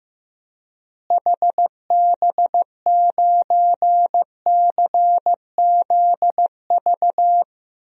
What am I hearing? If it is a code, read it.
Morse code, heard as a keyed tone, about 15 wpm, 700 Hz. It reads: HB9CZV
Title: DXCC-100 Call Signs - 6 Characters - Repeated 3x - 15wpm